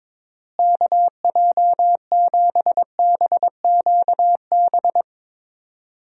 Morse code, heard as a keyed tone, about 22 wpm, 700 Hz. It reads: KJ7BQB
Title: USA Call Signs - 6 Characters - Repeated 3x - 22wpm